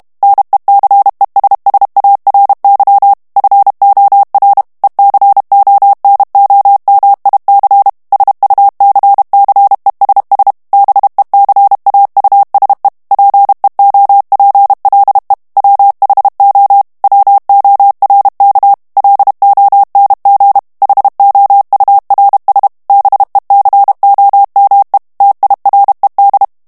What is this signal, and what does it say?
Morse code, heard as a keyed tone, about 32 wpm, 800 Hz. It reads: NECESSARY FOR ECONOMIC SUCCESS BECAUSE PEOPLE WHO WORK LONG HOURS BECOME TIRED
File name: morse mp3 / ANGLAIS texte clair 1 vitesse 30 mots minute